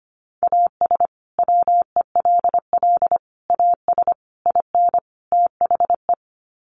Morse code, heard as a keyed tone, about 25 wpm, 700 Hz. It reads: AS WELL AS IN THE
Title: Sets of 5 Words Encore 25wpm